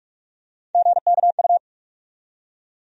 Morse code, heard as a keyed tone, about 45 wpm, 700 Hz. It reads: GKU
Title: USA Call Sign Postfixes - 3 Letters 45wpm